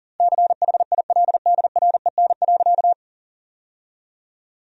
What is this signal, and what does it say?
Morse code, heard as a keyed tone, about 37 wpm, 700 Hz. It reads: CHILDREN.
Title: 4x Word-Spacing - Sentences from Top 300 Words - 40wpm